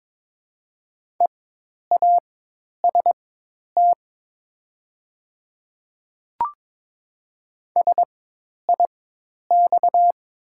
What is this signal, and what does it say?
Morse code, heard as a keyed tone, about 23 wpm, spaced as for 10 wpm, 700 Hz. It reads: EAST SIX
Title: 4x Character Spacing - Top 500 Words - 22wpm